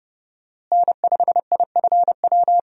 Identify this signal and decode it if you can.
Morse code, heard as a keyed tone, about 31 wpm, 700 Hz. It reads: N5IFW
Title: USA Call Signs - 5 Characters - Repeated 2x - 30wpm